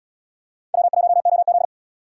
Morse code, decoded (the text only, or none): W1OZ